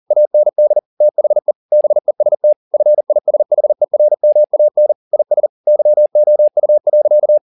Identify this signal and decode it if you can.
Morse code, heard as a keyed tone, about 40 wpm, 600 Hz. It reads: AND THE BEST FISHERMAN IS YOU.